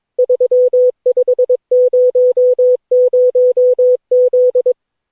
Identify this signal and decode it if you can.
Morse code, heard as a keyed tone, about 22 wpm, 500 Hz. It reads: 3500Z
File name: cw / Session9 cw training